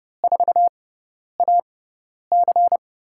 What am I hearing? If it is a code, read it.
Morse code, heard as a keyed tone, about 30 wpm, spaced as for 10 wpm, 700 Hz. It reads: 4AC
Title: USA Call Signs - 5 Characters - 6x Character Spacing - 30wpm